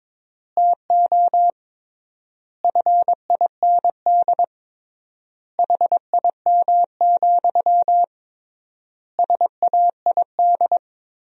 Morse code, heard as a keyed tone, about 22 wpm, 700 Hz. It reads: TO FIND HIM, SAID
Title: 3x Word-Spacing - Sentences from Top 400 Words - 22wpm